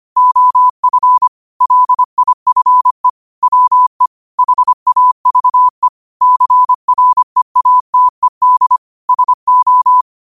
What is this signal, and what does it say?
Morse code, heard as a keyed tone, about 25 wpm, 1000 Hz. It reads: OF LIFE WE HAVE CREATED SO
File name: qotdfiles / 25wpm